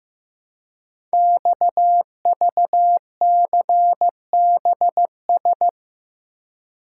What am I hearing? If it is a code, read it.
Morse code, heard as a keyed tone, about 15 wpm, 700 Hz. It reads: XVCBS
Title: Code Groups - 5 Letters - 15wpm